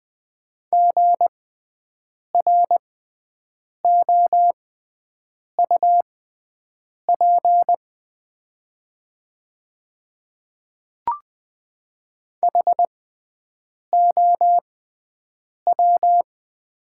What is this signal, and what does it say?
Morse code, heard as a keyed tone, about 20 wpm, spaced as for 7 wpm, 700 Hz. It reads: GROUP HOW
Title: 6x Character Spacing - Top 300 Words - 20wpm